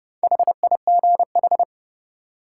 Morse code, heard as a keyed tone, about 30 wpm, 700 Hz. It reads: HIGH
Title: Sentences from Top 200 Words - New Word Drill - 30wpm